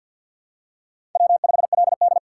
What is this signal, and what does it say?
Morse code, heard as a keyed tone, about 52 wpm, 700 Hz. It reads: W5LD
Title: USA Call Signs - 2x Character Spacing - 50wpm